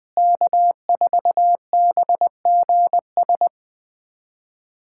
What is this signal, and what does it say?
Morse code, heard as a keyed tone, about 20 wpm, 700 Hz. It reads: K4BGS
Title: USA Callsign DE Callsign - 5 Characters - 20wpm